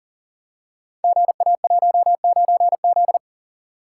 Morse code, heard as a keyed tone, about 39 wpm, 700 Hz. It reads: GA19Z
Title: Code Groups - 5 Letter:Numbers - 40wpm